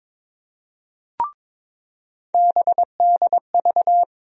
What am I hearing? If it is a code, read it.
Morse code, heard as a keyed tone, about 22 wpm, 700 Hz. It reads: BDV
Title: USA Call Sign Postfixes - 3 Letters 22wpm